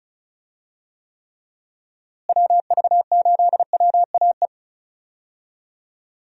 Morse code, heard as a keyed tone, about 36 wpm, 700 Hz. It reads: WV8WAE
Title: USA Call Signs - 6 Characters - Repeated 2x - 35wpm